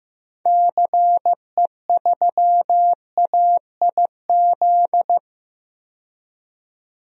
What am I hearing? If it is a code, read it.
Morse code, heard as a keyed tone, about 15 wpm, 700 Hz. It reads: CE3AIZ
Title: DXCC-100 Call Signs - 6 Characters - 15wpm